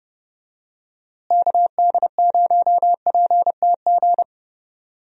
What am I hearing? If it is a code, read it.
Morse code, heard as a keyed tone, about 31 wpm, 700 Hz. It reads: KD0PTG
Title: USA Call Signs - 6 Characters - Repeated 3x - 30wpm